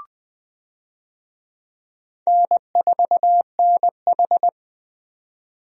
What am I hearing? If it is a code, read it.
Morse code, heard as a keyed tone, about 20 wpm, 700 Hz. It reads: N4NH
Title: USA Call Signs - 4 Characters - 20wpm